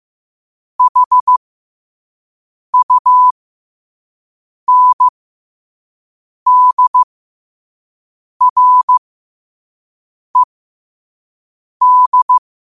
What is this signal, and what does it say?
Morse code, heard as a keyed tone, about 15 wpm, spaced as for 5 wpm, 1000 Hz. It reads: HUNDRED
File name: qotdfiles / 5wpm